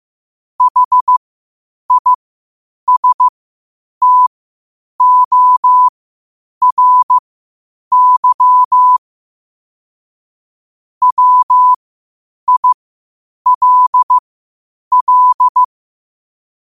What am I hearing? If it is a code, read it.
Morse code, heard as a keyed tone, about 15 wpm, spaced as for 8 wpm, 1000 Hz. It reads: HISTORY WILL
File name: qotdfiles / 8wpm